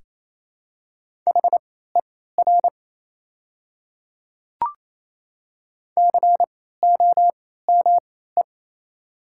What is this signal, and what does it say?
Morse code, heard as a keyed tone, about 28 wpm, spaced as for 16 wpm, 700 Hz. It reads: HER COME
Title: 3x Character Spacing - Top 100 Words - 28wpm